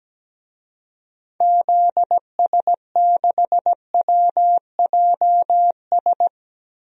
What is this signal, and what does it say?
Morse code, heard as a keyed tone, about 17 wpm, 700 Hz. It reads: ZS6WJS
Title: DXCC-100 Call Signs - 6 Characters - Repeated 3x - 17wpm